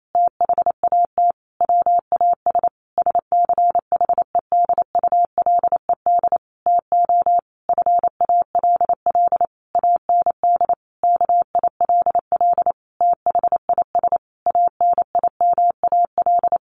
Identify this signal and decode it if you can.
Morse code, heard as a keyed tone, about 28 wpm, 700 Hz. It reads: THAT WAS SCHEDULED TO FALL AND KILL THIS ANIMAL
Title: A Sound Of Thunder 28wpm